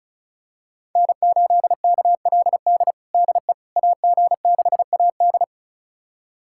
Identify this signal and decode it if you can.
Morse code, heard as a keyed tone, about 36 wpm, 700 Hz. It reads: N8KLD DE AG6AD
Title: USA Callsign DE Callsign - 5 Characters - 35wpm